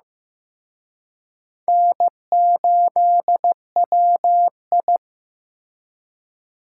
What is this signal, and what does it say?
Morse code, heard as a keyed tone, about 18 wpm, 700 Hz. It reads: N8WI
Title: Speed Racing - USA 1x2 Call Signs 15wpm